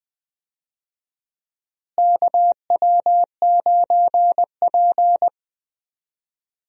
Morse code, heard as a keyed tone, about 20 wpm, 700 Hz. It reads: KW9P
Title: USA Call Signs - 4 Characters - Repeated 3x - 20wpm